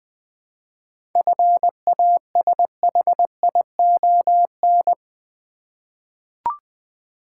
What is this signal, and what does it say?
Morse code, heard as a keyed tone, about 20 wpm, 690 Hz. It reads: FASHION
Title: 7 Letter Words - First 3 Letters Given - 20wpm